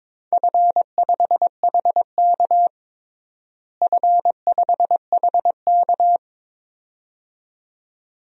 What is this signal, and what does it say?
Morse code, heard as a keyed tone, about 22 wpm, 700 Hz. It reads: F5HK F5HK
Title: DXCC-100 Call Signs - 4 Characters - Repeated 2x - 22wpm